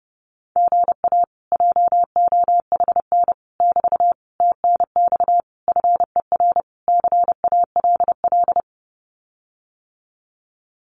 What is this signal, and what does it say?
Morse code, heard as a keyed tone, about 30 wpm, 700 Hz. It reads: GA JOHN <BT> TNX FER CALL
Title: Common QSO Phrases Encore 30wpm